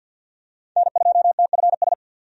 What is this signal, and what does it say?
Morse code, heard as a keyed tone, about 50 wpm, 700 Hz. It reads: N2TFS
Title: USA Call Signs - 5 Characters - 50wpm